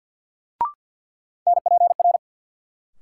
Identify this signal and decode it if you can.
Morse code, heard as a keyed tone, about 52 wpm, 700 Hz. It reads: NPR
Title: Sets of 5 Words - New Word Drill - 50wpm